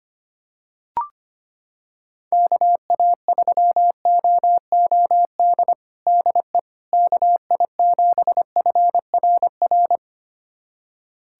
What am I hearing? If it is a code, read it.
Morse code, heard as a keyed tone, about 25 wpm, 700 Hz. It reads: KA3OOD DE KI7FRR
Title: USA Callsign DE Callsign 25wpm